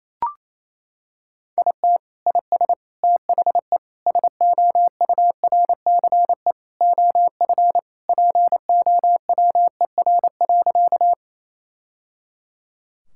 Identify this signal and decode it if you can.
Morse code, heard as a keyed tone, about 28 wpm, 700 Hz. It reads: IT IS THE SOURCE OF POWER.
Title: Sentences from Top 1000 Words - Review Part 5 - 28wpm